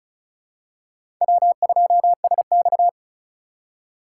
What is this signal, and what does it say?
Morse code, heard as a keyed tone, about 34 wpm, 700 Hz. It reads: W2SX
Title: USA Call Signs - 4 Characters - 4x Character Spacing - 35wpm